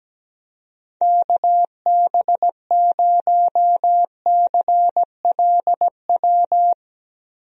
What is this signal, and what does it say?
Morse code, heard as a keyed tone, about 17 wpm, 700 Hz. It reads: KB0CLW
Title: USA Call Signs - 6 Characters - 17wpm